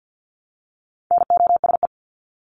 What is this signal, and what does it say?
Morse code, heard as a keyed tone, about 48 wpm, 700 Hz. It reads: NOSE
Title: Top 1501-1750 Words 50wpm